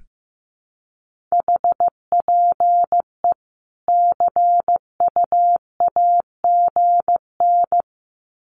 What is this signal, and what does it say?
Morse code, heard as a keyed tone, about 15 wpm, 700 Hz. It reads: HPE CUAGN
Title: Common QSO Phrases 15wpm